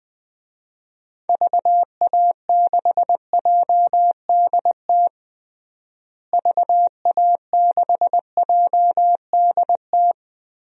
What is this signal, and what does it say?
Morse code, heard as a keyed tone, about 20 wpm, 700 Hz. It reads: VA6JDT VA6JDT
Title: DXCC-100 Call Signs - 6 Characters - Repeated 3x - 20wpm